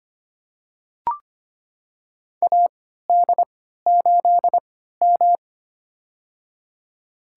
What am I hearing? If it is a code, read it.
Morse code, heard as a keyed tone, about 25 wpm, spaced as for 14 wpm, 700 Hz. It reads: AD8M
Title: USA Call Signs - 4 Characters - 3x Character Spacing - 25wpm